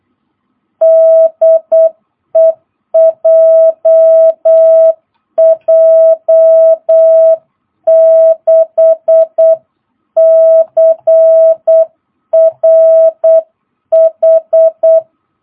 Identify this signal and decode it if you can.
Morse code, heard as a keyed tone, about 8 wpm, 660 Hz. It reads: DE JJ6CRH